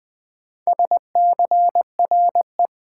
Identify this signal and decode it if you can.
Morse code, heard as a keyed tone, about 21 wpm, 700 Hz. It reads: SCRE
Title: Sentences from Top 1600 Words - New Word Drill - 20wpm